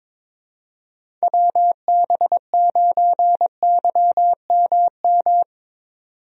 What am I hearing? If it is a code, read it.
Morse code, heard as a keyed tone, about 22 wpm, 700 Hz. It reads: WB9YMM
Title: USA Call Signs - 6 Characters - 22wpm